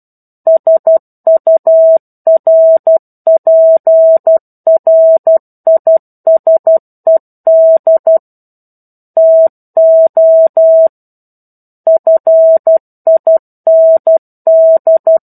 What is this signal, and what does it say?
Morse code, heard as a keyed tone, about 12 wpm, 650 Hz. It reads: SURPRISED TO FIND
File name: 01-Chapter-01 12 WPM